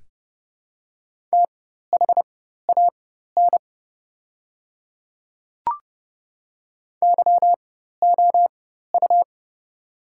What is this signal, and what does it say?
Morse code, heard as a keyed tone, about 30 wpm, spaced as for 14 wpm, 700 Hz. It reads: THAN YOU
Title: 4x Character Spacing - Top 300 Words - 30wpm